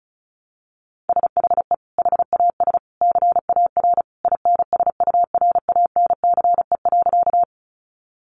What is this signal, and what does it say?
Morse code, heard as a keyed tone, about 34 wpm, 700 Hz. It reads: SHE HAS CAR INSURANCE.
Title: Sentences from Top 1100 Words 35wpm